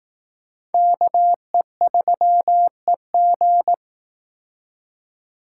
Morse code, heard as a keyed tone, about 18 wpm, 700 Hz. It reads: KE3EG
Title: Speed Racing - USA 2x2 Call Signs 15wpm